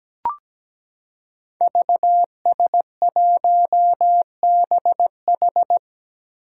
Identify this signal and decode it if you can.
Morse code, heard as a keyed tone, about 17 wpm, 700 Hz. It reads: VS1BH
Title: Five Challenging Letter-Number Combinations 17wpm